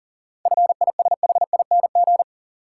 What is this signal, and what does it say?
Morse code, heard as a keyed tone, about 40 wpm, 700 Hz. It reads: FISHING
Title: Seven Letter Words - 40wpm